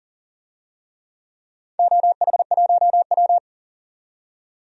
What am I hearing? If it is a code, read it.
Morse code, heard as a keyed tone, about 40 wpm, 700 Hz. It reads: OH1W
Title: DXCC-100 Call Signs - 4 Characters - 40wpm